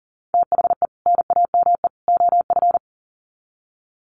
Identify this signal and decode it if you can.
Morse code, heard as a keyed tone, about 40 wpm, 700 Hz. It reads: THE NAME OF
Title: Sets of 3 Words 40wpm